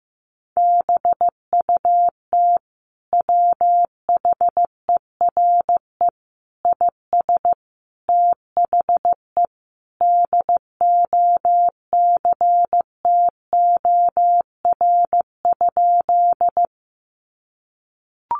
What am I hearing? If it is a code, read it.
Morse code, heard as a keyed tone, about 15 wpm, 700 Hz. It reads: BUT WHERE IS THE DOCTOR?
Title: Sentences from Top 1500 Words 15wpm